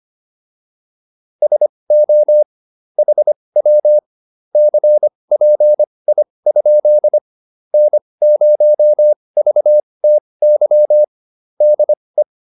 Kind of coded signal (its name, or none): Morse code